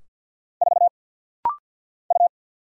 Morse code, heard as a keyed tone, about 50 wpm, 700 Hz. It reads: 4 U
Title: Single Letter-Number - Rapid-Fire 50wpm